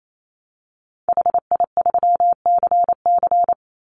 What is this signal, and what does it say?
Morse code, heard as a keyed tone, about 28 wpm, 700 Hz. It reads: HI3CC
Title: Call Signs - 28wpm